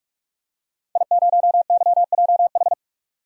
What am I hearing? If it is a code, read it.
Morse code, heard as a keyed tone, about 44 wpm, 700 Hz. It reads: I0YJH
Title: DXCC-100 Call Signs - 5 Characters - 45wpm